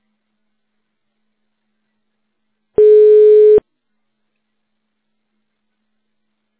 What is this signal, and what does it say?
Ringback tone (425 Hz, 1 s on, 3 s off).